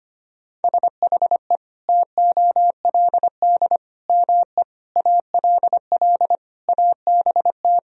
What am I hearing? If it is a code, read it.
Morse code, heard as a keyed tone, about 25 wpm, 700 Hz. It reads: SHE TOLD ME ALL ABT
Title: Sentences from Top 400 Words and Abbreviations 25wpm